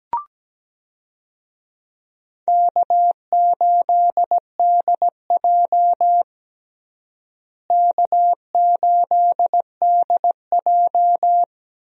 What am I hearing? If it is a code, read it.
Morse code, heard as a keyed tone, about 17 wpm, 700 Hz. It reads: K8DJ K8DJ
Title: USA Call Signs - 4 Characters - Repeated 2x - 17wpm